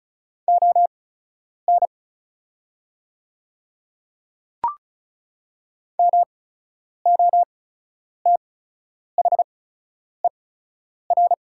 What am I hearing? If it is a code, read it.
Morse code, heard as a keyed tone, about 36 wpm, spaced as for 10 wpm, 700 Hz. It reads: ON MOTHER
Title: 8x Character Spacing - Top 300 Words - 35wpm